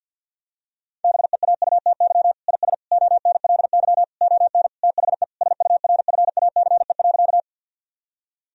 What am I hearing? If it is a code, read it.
Morse code, heard as a keyed tone, about 47 wpm, 700 Hz. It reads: BEAUTY IS ONLY ON THE SURFACE.
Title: Sentences from Top 1000 Words - Review Part 3 - 50wpm